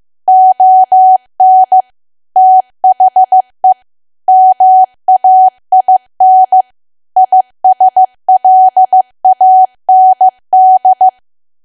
Morse code, heard as a keyed tone, about 15 wpm, 750 Hz. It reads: ON THE MAIN ISLAND
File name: cwa / W1AW-Level2 / 15WPM